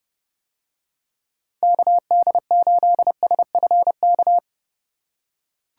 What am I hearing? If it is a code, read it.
Morse code, heard as a keyed tone, about 30 wpm, 700 Hz. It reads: KD8SFK